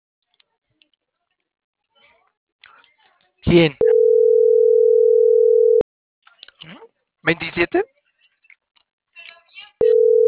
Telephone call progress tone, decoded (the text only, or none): ringback tone